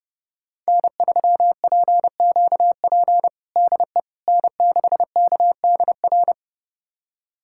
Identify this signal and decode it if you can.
Morse code, heard as a keyed tone, about 32 wpm, 700 Hz. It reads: N3PQP DE N6KDR
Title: USA Callsign DE Callsign - 5 Characters - 30wpm